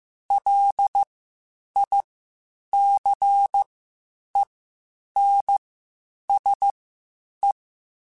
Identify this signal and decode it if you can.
Morse code, heard as a keyed tone, about 15 wpm, spaced as for 8 wpm, 800 Hz. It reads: LICENSE
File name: (spaced) / QSOs Spaced-mp3s / WA7QOX 2 @13 sp